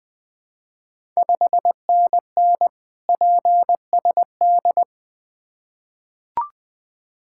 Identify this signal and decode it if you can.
Morse code, heard as a keyed tone, about 21 wpm, 690 Hz. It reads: 5NN PSD
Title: Ontario QSO Party Exchange 20wpm